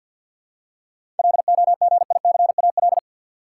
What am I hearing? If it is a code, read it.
Morse code, heard as a keyed tone, about 52 wpm, 700 Hz. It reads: LOGICAL
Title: Seven Letter Words - 50wpm